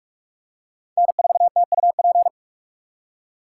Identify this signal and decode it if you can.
Morse code, heard as a keyed tone, about 40 wpm, 700 Hz. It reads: N4TUP